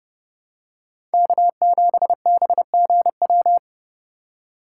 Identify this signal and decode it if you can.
Morse code, heard as a keyed tone, about 28 wpm, 700 Hz. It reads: K7BGW